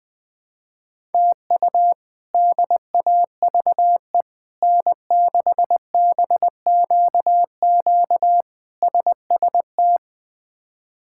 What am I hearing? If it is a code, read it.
Morse code, heard as a keyed tone, about 21 wpm, 700 Hz. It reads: TU DAVE N6BQQ SST
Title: K1USN SST QSO Practice - 20wpm